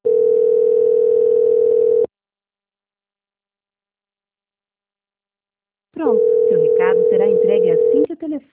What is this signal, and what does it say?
Ringback tone (North America), 440 + 480 Hz, with speech.
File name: out- -2002- - - .18